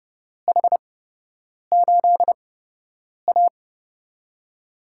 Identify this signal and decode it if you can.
Morse code, heard as a keyed tone, about 31 wpm, spaced as for 8 wpm, 700 Hz. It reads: H8A
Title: USA Call Signs - 8x Character Spacing - 30wpm